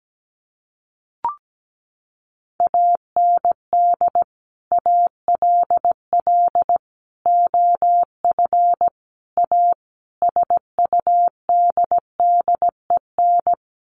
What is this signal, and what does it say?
Morse code, heard as a keyed tone, about 17 wpm, 700 Hz. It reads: AND ALL OF A SUDDEN
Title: Sets of 5 Words 17wpm